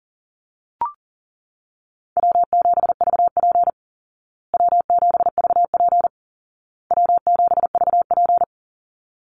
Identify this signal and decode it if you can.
Morse code, heard as a keyed tone, about 37 wpm, 700 Hz. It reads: W7VP W7VP W7VP
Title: Easy Call Signs Repeated 3x - 40wpm